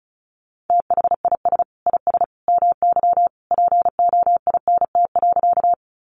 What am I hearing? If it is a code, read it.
Morse code, heard as a keyed tone, about 34 wpm, 700 Hz. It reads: THIS IS MY POINT.
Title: Sentences from Top 200 Words 35wpm